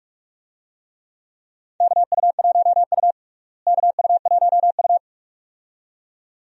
Morse code, heard as a keyed tone, about 46 wpm, 700 Hz. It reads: KU1U KU1U
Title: USA Call Signs - 4 Characters - Repeated 2x - 45wpm